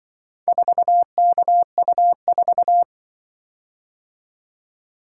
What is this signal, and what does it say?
Morse code, heard as a keyed tone, about 25 wpm, 700 Hz. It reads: 4KU4
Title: Speed-Racing - Four Letter-Number Combinations - 20wpm